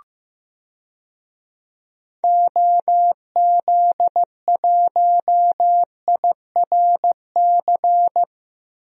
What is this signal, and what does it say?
Morse code, heard as a keyed tone, about 15 wpm, 700 Hz. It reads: OZ1IRC
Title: DXCC-100 Call Signs - 6 Characters - 15wpm